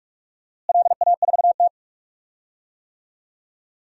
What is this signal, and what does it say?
Morse code, heard as a keyed tone, about 46 wpm, 700 Hz. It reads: LA4T
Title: DXCC-100 Call Signs - 4 Characters - Repeated 3x - 45wpm